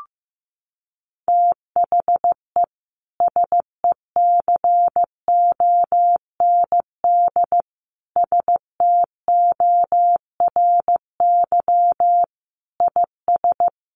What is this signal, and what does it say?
Morse code, heard as a keyed tone, about 15 wpm, 700 Hz. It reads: THE SECOND STORY IS
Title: Sentences from Top 1400 Words 15wpm